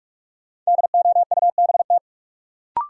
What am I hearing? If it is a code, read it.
Morse code, heard as a keyed tone, about 42 wpm, 700 Hz. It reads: DOUBT
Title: Sentences from Top 1500 Words - New Word Drill Alt - 45wpm